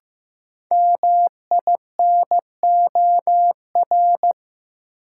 Morse code, heard as a keyed tone, about 15 wpm, 700 Hz. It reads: MINOR
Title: Sentences from Top 1500 Words - New Word Drill Alt - 15wpm